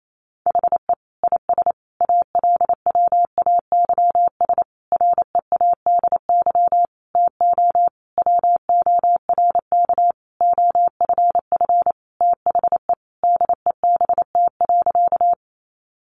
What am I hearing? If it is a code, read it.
Morse code, heard as a keyed tone, about 28 wpm, 700 Hz. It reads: HE IS ALWAYS READY TO WORK OFF THE DEBT.
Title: Sentences from Top 1300 Words 28wpm